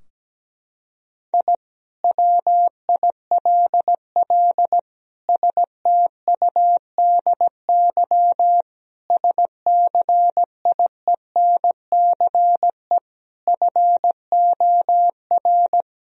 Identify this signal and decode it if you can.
Morse code, heard as a keyed tone, about 17 wpm, 700 Hz. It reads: I WILL STUDY SCIENCE FOR
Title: Sentences from Top 1000 Words - Review Part 2 - 17wpm